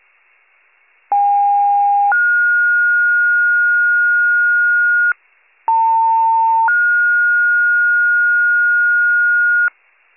Two-tone page, heard as a one-second tone, then a three-second tone, about 810 then 1430 Hz, about 900 then 1430 Hz. It is heard from 1.1 to 9.7 s.